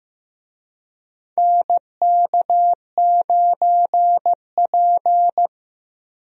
Morse code, heard as a keyed tone, about 15 wpm, 700 Hz. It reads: NK9P
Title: USA Call Signs - 4 Characters - Repeated 2x - 15wpm